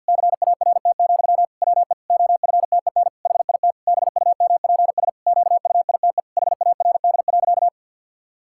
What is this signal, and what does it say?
Morse code, heard as a keyed tone, about 47 wpm, 700 Hz. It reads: CART, WE OFTEN HIT BUMPS QUITE HARD.
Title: Pangram Challenge v2 - 50wpm